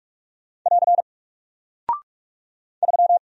Morse code, heard as a keyed tone, about 46 wpm, 700 Hz. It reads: <AR> 3
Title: Single Character (Letter:Number:Punctuation:Prosign) Alt 45wpm